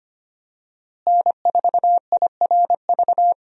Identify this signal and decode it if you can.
Morse code, heard as a keyed tone, about 25 wpm, 700 Hz. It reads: N4IRV
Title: USA Call Signs - 5 Characters - Repeated 2x - 25wpm